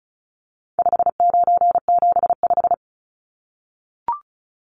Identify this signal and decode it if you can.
Morse code, heard as a keyed tone, about 34 wpm, 700 Hz. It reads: 5975
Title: Numbers - 4-Digit 35wpm